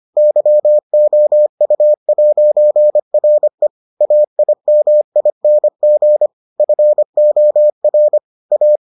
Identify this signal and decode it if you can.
Morse code, heard as a keyed tone, about 25 wpm, 600 Hz. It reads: YOU'RE AIMING FOR A